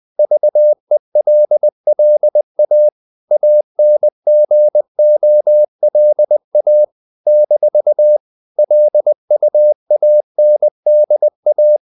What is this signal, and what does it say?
Morse code, heard as a keyed tone, about 20 wpm, 600 Hz. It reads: VELLA ANGOLA - LUANDA
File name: CW / Practice / NationalCapitals20